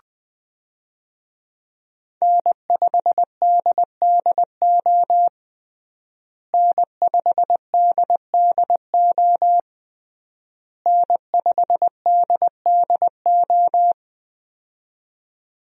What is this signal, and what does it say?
Morse code, heard as a keyed tone, about 21 wpm, 700 Hz. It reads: N5DDO N5DDO N5DDO
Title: USA Call Signs - 5 Characters - Repeated 3x - 20wpm